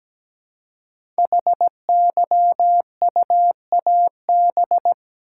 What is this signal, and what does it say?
Morse code, heard as a keyed tone, about 17 wpm, 700 Hz. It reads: HYUAB
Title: Code Groups - 5 Letter:Numbers - 17wpm